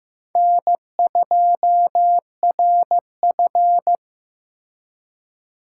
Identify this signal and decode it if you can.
Morse code, heard as a keyed tone, about 18 wpm, 700 Hz. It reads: N2RF
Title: Speed Racing - USA 1x2 Call Signs 15wpm